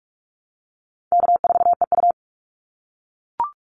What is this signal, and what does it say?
Morse code, heard as a keyed tone, about 44 wpm, 700 Hz. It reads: K4EU
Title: Call Signs - 45wpm